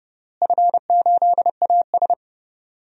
Morse code, heard as a keyed tone, about 32 wpm, 700 Hz. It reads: F8AS
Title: DXCC-100 Call Signs - 4 Characters - Repeated 3x - 30wpm